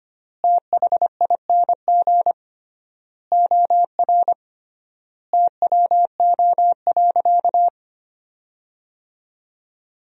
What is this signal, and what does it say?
Morse code, heard as a keyed tone, about 25 wpm, 700 Hz. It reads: THING OR TWO.
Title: 3x Word-Spacing - Sentences from Top 300 Words - 25wpm